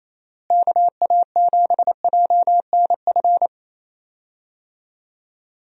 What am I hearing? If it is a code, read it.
Morse code, heard as a keyed tone, about 27 wpm, 700 Hz. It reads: KA7JNF
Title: USA Call Signs - 6 Characters - Repeated 2x - 28wpm